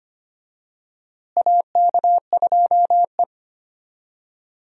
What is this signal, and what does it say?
Morse code, heard as a keyed tone, about 25 wpm, 700 Hz. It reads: AK2E
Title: USA Call Signs - 4 Characters - Repeated 2x - 25wpm